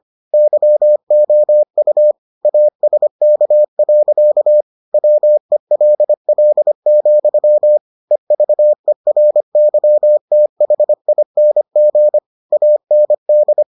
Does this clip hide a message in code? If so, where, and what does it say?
Yes: Morse code from 0.3 to 13.6 s, heard as a keyed tone, about 25 wpm, 600 Hz. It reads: YOU ASK. WELL, EVERYTHING AND